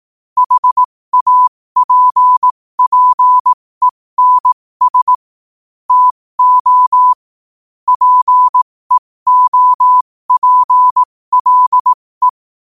Morse code, heard as a keyed tone, about 18 wpm, spaced as for 15 wpm, 1000 Hz. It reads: HAPPENS TO PEOPLE
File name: qotdfiles / 15wpm